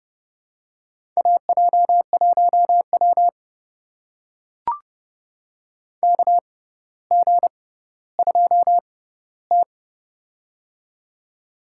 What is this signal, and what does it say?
Morse code, heard as a keyed tone, about 30 wpm, spaced as for 15 wpm, 700 Hz. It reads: AJ1W KG2T
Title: USA Call Signs - 4 Characters - 6x Character Spacing - 30wpm